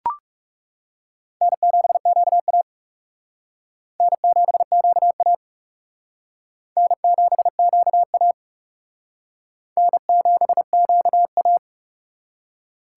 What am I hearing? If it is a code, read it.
Morse code, heard as a keyed tone, about 37 wpm, 700 Hz. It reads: N7QA N7QA N7QA N7QA
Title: Speed Racing - USA 1x2 Call Signs 30wpm